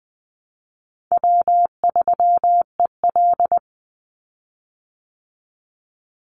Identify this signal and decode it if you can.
Morse code, heard as a keyed tone, about 20 wpm, 700 Hz. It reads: W3EL
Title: Call Signs Repeated 2x - 20wpm